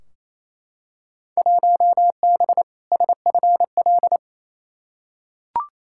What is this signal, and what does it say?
Morse code, heard as a keyed tone, about 28 wpm, 700 Hz. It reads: 1B SFL
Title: Field Day Exchanges Encore - 28wpm